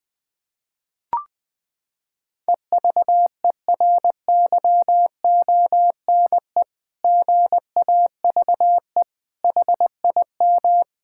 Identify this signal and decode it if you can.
Morse code, heard as a keyed tone, about 20 wpm, 700 Hz. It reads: EVERYONE GAVE HIM
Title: Sentences from Top 1000 Words - Review Part 6 - 20wpm